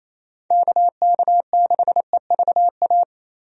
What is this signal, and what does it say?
Morse code, heard as a keyed tone, about 28 wpm, 700 Hz. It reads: KK6EVA
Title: USA Call Signs - 6 Characters - Repeated 2x - 28wpm